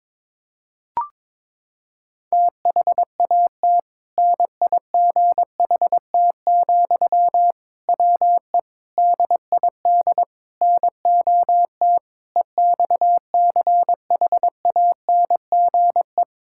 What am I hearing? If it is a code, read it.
Morse code, heard as a keyed tone, about 22 wpm, 700 Hz. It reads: THAT NIGHT, WE DID NOT EXCHANGE
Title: Sentences from Top 1000 Words - Review Part 6 - 22wpm